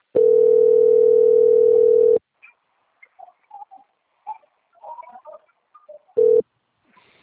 Ringback tone (North America), 440 + 480 Hz.